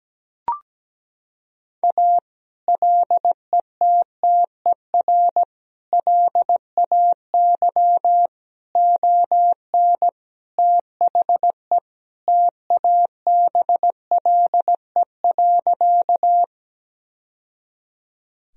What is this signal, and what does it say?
Morse code, heard as a keyed tone, about 17 wpm, 700 Hz. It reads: A LETTER LAY ON THE TABLE.
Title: Sentences from Top 1000 Words - Review Part 2 - 17wpm